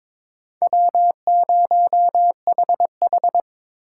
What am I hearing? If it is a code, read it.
Morse code, heard as a keyed tone, about 22 wpm, 700 Hz. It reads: W0HH
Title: USA Call Signs - 4 Characters - 4x Character Spacing - 22wpm